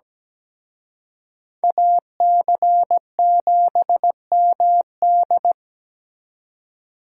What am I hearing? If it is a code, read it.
Morse code, heard as a keyed tone, about 20 wpm, 700 Hz. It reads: AC7MD
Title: Speed Racing - USA 2x2 Call Signs 17wpm